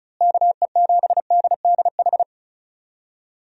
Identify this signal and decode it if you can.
Morse code, heard as a keyed tone, about 34 wpm, 700 Hz. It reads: KE7DDH
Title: USA Call Signs - 6 Characters - Repeated 3x - 35wpm